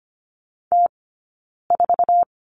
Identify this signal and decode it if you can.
Morse code, heard as a keyed tone, about 25 wpm, 700 Hz. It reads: T 4
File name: Lesson 17d - Letter u - Two Characters 25wpm